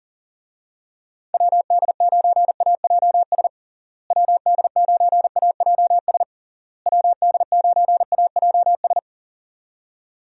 Morse code, heard as a keyed tone, about 37 wpm, 700 Hz. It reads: WD9AJS WD9AJS WD9AJS
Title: USA Call Signs - 6 Characters - Repeated 3x - 40wpm